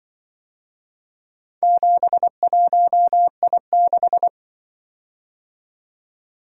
Morse code, heard as a keyed tone, about 25 wpm, 700 Hz. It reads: 71I6
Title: Speed-Racing - Four Letter-Number Combinations - 20wpm